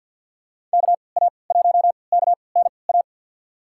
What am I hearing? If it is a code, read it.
Morse code, heard as a keyed tone, about 47 wpm, spaced as for 28 wpm, 700 Hz. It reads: KA1KNA
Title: USA Call Signs - 6 Characters - 3x Character Spacing - 50wpm